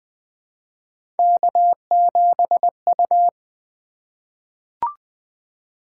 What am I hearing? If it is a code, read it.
Morse code, heard as a keyed tone, about 20 wpm, 700 Hz. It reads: K7U
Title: Three Letter-Number Combinations - 20wpm